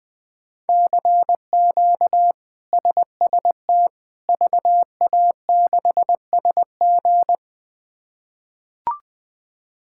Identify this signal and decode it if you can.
Morse code, heard as a keyed tone, about 20 wpm, 700 Hz. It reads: CQ SST VA6SG
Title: K1USN SST QSO Practice - 20wpm